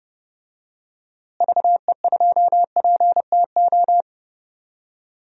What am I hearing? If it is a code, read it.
Morse code, heard as a keyed tone, about 30 wpm, 700 Hz. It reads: VE2PTO
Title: DXCC-100 Call Signs - 6 Characters - Repeated 3x - 30wpm